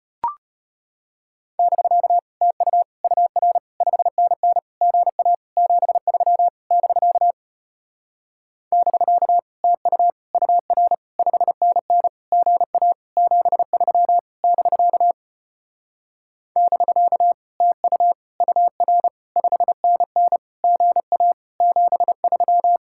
Morse code, heard as a keyed tone, about 33 wpm, 700 Hz. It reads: <BK> TU UR 5NN GA 73 <BK> <BK> TU UR 5NN GA 73 <BK> <BK> TU UR 5NN GA 73